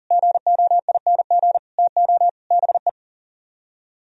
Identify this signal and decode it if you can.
Morse code, heard as a keyed tone, about 51 wpm, 700 Hz. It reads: GOING TO BE